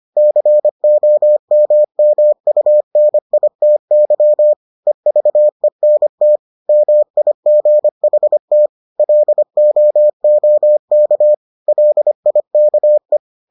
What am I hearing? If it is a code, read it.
Morse code, heard as a keyed tone, about 25 wpm, 600 Hz. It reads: COMMUNITY EVENT MIGHT LOOK LIKE